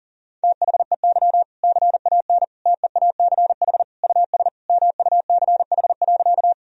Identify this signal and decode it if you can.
Morse code, heard as a keyed tone, about 40 wpm, 700 Hz. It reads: THEY CAN TEACH US MUCH.
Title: Sentences from Top 1000 Words - Review Part 2 - 40wpm